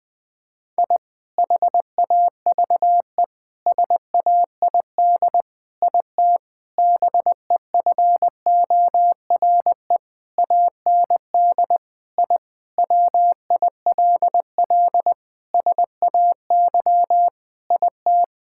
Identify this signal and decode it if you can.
Morse code, heard as a keyed tone, about 20 wpm, 700 Hz. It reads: I HAVE SAID IT BEFORE AND I WILL SAY IT
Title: Sentences with 8-Word Phrases - 20wpm